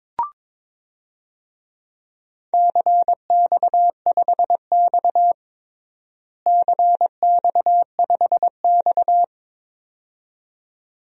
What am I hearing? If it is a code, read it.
Morse code, heard as a keyed tone, about 22 wpm, 700 Hz. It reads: CX5X CX5X
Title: DXCC-100 Call Signs - 4 Characters - Repeated 2x - 22wpm